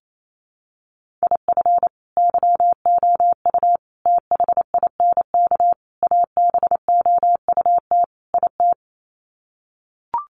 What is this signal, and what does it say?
Morse code, heard as a keyed tone, about 28 wpm, 700 Hz. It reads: IF YOU THINK ABOUT IT
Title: Sets of 5 Words Encore2 28wpm